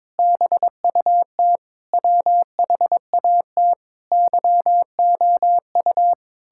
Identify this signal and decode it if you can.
Morse code, heard as a keyed tone, about 21 wpm, 700 Hz. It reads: BUT WHAT YOU
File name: Sentences with 9-Word Phrases - 22wpm